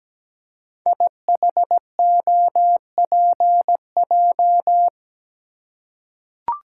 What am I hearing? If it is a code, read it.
Morse code, heard as a keyed tone, about 17 wpm, 700 Hz. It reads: IHOPJ
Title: Code Groups - 5 Letters - 17wpm